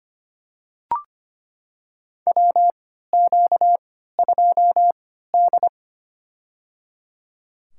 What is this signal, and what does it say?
Morse code, heard as a keyed tone, about 25 wpm, spaced as for 14 wpm, 700 Hz. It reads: WQ2D
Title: USA Call Signs - 4 Characters - 3x Character Spacing - 25wpm